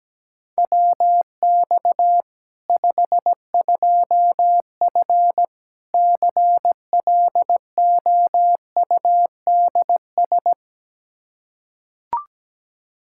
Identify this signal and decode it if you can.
Morse code, heard as a keyed tone, about 17 wpm, 700 Hz. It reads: WX 52F CLOUDS
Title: K1USN SST QSO Practice 17wpm